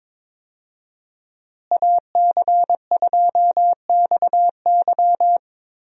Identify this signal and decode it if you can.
Morse code, heard as a keyed tone, about 22 wpm, 700 Hz. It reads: AC2XY
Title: USA Call Signs - 5 Characters - Repeated 2x - 22wpm